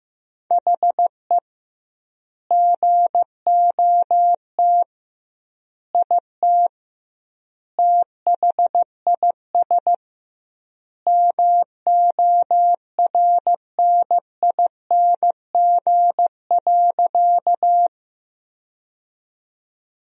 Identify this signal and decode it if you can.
Morse code, heard as a keyed tone, about 15 wpm, 700 Hz. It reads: HE GOT IT THIS MORNING.
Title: 2x Word-Spacing - Sentences from Top 400 Words - 15wpm